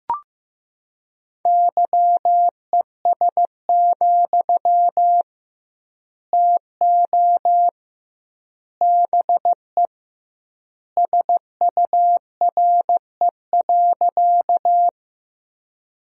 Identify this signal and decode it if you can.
Morse code, heard as a keyed tone, about 15 wpm, 700 Hz. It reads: YES, TO BE SURE.
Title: 2x Word-Spacing - Sentences from Top 500 Words - 15wpm